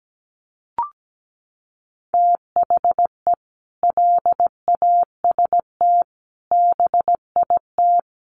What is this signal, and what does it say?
Morse code, heard as a keyed tone, about 17 wpm, 700 Hz. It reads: THE LAST BIT
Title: Sentences from Top 900 Words 17wpm